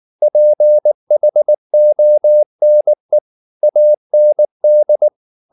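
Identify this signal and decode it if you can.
Morse code, heard as a keyed tone, about 19 wpm, 600 Hz. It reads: PHONE AND
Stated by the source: Morse code at 19 wpm from CW med Gnister, reading Ken Follett's novel Never in English.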